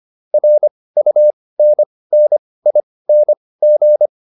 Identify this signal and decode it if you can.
Morse code, heard as a keyed tone, about 25 wpm, spaced as for 18 wpm, 600 Hz. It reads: RUNNING
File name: QSO408 18